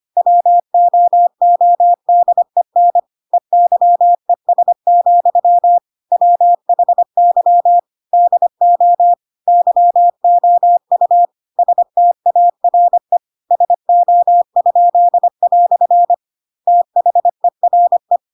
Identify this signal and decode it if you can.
Morse code, heard as a keyed tone, about 25 wpm, 700 Hz. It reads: WOODEN EYES, WHY DO YOU STARE SO?" THERE
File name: Pinocchio / Pinocchio ch3 25 25 ik1hge